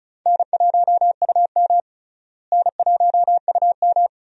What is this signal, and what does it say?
Morse code, heard as a keyed tone, about 36 wpm, 700 Hz. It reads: N1UM N1UM
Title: USA Call Signs - 4 Characters - Repeated 2x - 35wpm